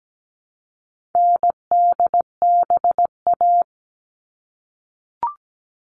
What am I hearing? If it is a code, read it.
Morse code, heard as a keyed tone, about 17 wpm, 700 Hz. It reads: NDBA
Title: All Cut Numbers - 4-Digit 17wpm